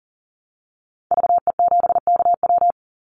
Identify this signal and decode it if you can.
Morse code, heard as a keyed tone, about 40 wpm, 700 Hz. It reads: VE7KW
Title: Call Signs Repeated 2x - 40wpm